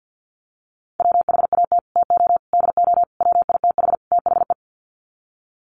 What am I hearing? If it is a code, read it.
Morse code, heard as a keyed tone, about 50 wpm, 700 Hz. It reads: WHAT TO DO WITH THE
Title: Sets of 5 Words Encore2 50wpm